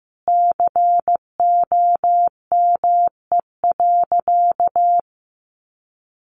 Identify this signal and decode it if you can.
Morse code, heard as a keyed tone, about 15 wpm, 700 Hz. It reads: COME.
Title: Sentences from Top 100 Words 15wpm